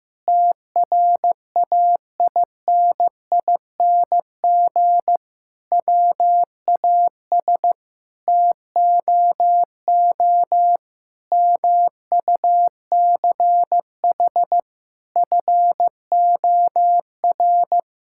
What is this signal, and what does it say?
Morse code, heard as a keyed tone, about 16 wpm, 700 Hz. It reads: TRAINING WAS TOO MUCH FOR
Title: Sentences from Top 1000 Words - Review Part 5 - 15wpm